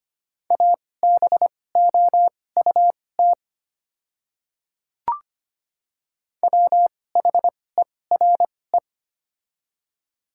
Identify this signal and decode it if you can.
Morse code, heard as a keyed tone, about 25 wpm, spaced as for 18 wpm, 700 Hz. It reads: ABOUT WHERE
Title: 2x Character Spacing - Top 300 Words - 25wpm